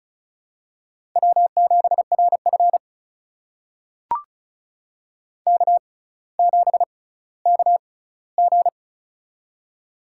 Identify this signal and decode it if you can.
Morse code, heard as a keyed tone, about 34 wpm, spaced as for 18 wpm, 700 Hz. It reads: W7RF K7KG
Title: USA Call Signs - 6x Character Spacing - 35wpm